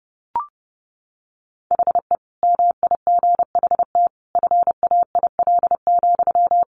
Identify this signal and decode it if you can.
Morse code, heard as a keyed tone, about 30 wpm, 700 Hz. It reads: HE MIGHT FAIL,
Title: Sentences from Top 1100 Words 30wpm